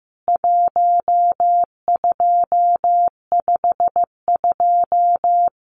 Morse code, heard as a keyed tone, about 15 wpm, 700 Hz. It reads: 1252
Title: CWT Exchange 15wpm